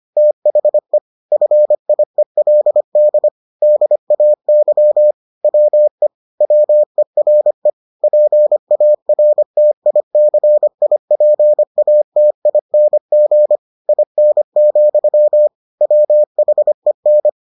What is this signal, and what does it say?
Morse code, heard as a keyed tone, about 25 wpm, 600 Hz. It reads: THE FIELD DAY WE WERE PARTICIPATING IN, WHEN